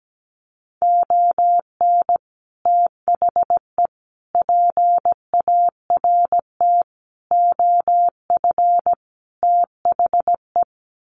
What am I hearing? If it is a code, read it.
Morse code, heard as a keyed tone, about 17 wpm, 700 Hz. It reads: ON THE PART OF THE
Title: Sets of 5 Words 17wpm